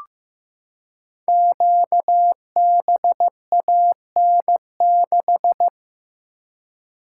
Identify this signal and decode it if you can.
Morse code, heard as a keyed tone, about 15 wpm, 700 Hz. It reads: QBAN6
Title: Code Groups - 5 Letter:Numbers - 15wpm